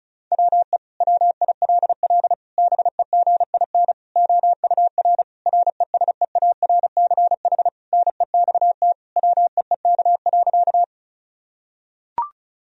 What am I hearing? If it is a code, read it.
Morse code, heard as a keyed tone, about 34 wpm, 700 Hz. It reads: WE WILL BEGIN OUR RESEARCH NEXT WEEK.
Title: Sentences from Top 1000 Words - Review Part 4 - 35wpm